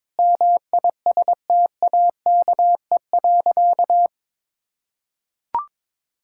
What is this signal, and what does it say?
Morse code, heard as a keyed tone, about 22 wpm, 700 Hz. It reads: MISTAKE.
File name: practice / 22wpm / Sentences with 4-Word Phrases - 22wpm